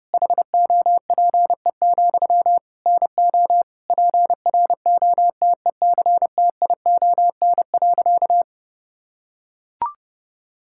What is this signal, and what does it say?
Morse code, heard as a keyed tone, about 30 wpm, 700 Hz. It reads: HOPE, NO PROTECTION.
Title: Sentences from Top 1600 Words 30wpm